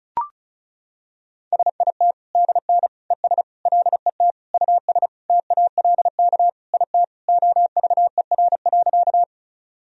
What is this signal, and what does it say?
Morse code, heard as a keyed tone, about 34 wpm, 700 Hz. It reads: SIT DN ES LET US TALK IT OVER.
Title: Sentences from Top 500 Words and Abbreviations 35wpm